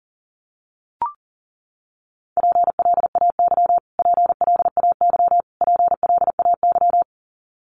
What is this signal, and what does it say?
Morse code, heard as a keyed tone, about 40 wpm, 700 Hz. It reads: PLAY PLAY PLAY
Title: Top 500 Words Repeated 40wpm